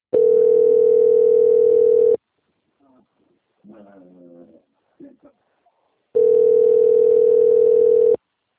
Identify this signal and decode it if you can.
Ringback tone (North America), 440 + 480 Hz, with speech.
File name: out- - - - - .19355